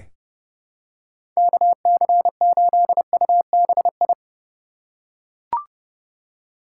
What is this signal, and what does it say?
Morse code, heard as a keyed tone, about 30 wpm, 700 Hz. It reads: KC8UBI
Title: USA Call Signs - 6 Characters - 3x Character Spacing - 30wpm